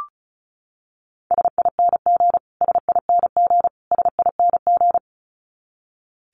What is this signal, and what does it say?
Morse code, heard as a keyed tone, about 34 wpm, 700 Hz. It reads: SING SING SING
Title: Top 500 Words Repeated 35wpm